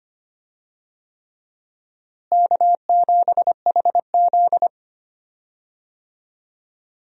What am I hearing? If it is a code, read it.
Morse code, heard as a keyed tone, about 25 wpm, 700 Hz. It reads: K7HZ
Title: USA Call Signs - 4 Characters - Repeated 3x - 25wpm